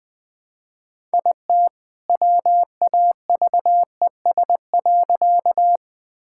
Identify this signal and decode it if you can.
Morse code, heard as a keyed tone, about 20 wpm, 700 Hz. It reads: IT WAVES.
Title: All-Age Appropriate Jokes - 20wpm